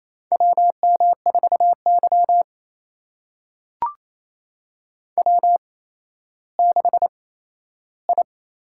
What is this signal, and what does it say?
Morse code, heard as a keyed tone, about 28 wpm, spaced as for 14 wpm, 700 Hz. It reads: WM4Y W6I
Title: USA Call Signs - 8x Character Spacing - 28wpm